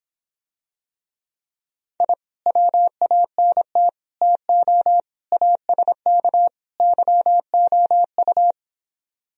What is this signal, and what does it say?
Morse code, heard as a keyed tone, about 26 wpm, 700 Hz. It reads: I WANT TO ASK YOU